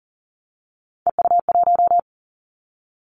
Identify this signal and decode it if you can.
Morse code, heard as a keyed tone, about 42 wpm, 700 Hz. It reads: EU1
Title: Callsign Prefixes for Top 100 DXCC - 40wpm